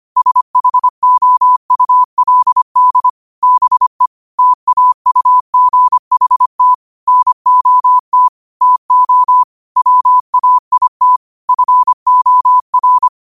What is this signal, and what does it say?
Morse code, heard as a keyed tone, about 25 wpm, 1000 Hz. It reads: SHOULD BE TAUGHT NOT TO WAIT FOR
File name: qotdfiles / 25wpm